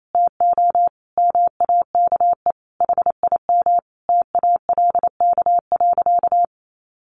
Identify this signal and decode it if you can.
Morse code, heard as a keyed tone, about 28 wpm, 700 Hz. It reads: TO MAKE HIM TALK.
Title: Sentences from Top 600 Words 28wpm